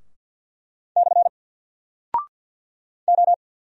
Morse code, heard as a keyed tone, about 49 wpm, 700 Hz. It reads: / O
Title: Single Character (Letter:Number:Punctuation:Prosign) 50wpm